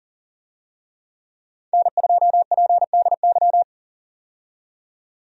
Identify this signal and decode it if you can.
Morse code, heard as a keyed tone, about 40 wpm, 700 Hz. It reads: N2PDY
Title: USA Call Signs - 5 Characters - 40wpm